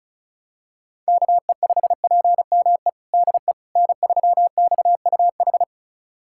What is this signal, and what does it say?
Morse code, heard as a keyed tone, about 36 wpm, 700 Hz. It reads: KE5PME DE N3XUH
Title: USA Callsign DE Callsign 35wpm